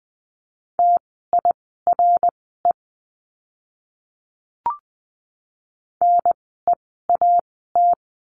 Morse code, heard as a keyed tone, about 20 wpm, spaced as for 14 wpm, 700 Hz. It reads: TIRE NEAT
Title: 054 Lesson 10f - Letter r - Words 20wpm